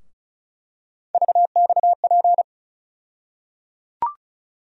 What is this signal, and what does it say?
Morse code, heard as a keyed tone, about 36 wpm, 700 Hz. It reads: VXP
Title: USA Call Sign Postfixes - 3 Letters 35wpm